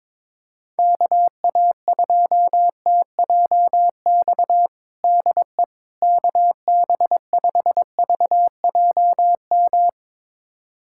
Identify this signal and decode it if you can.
Morse code, heard as a keyed tone, about 22 wpm, 700 Hz. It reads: KA2TJX DE KB5VJM
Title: USA Callsign DE Callsign 22wpm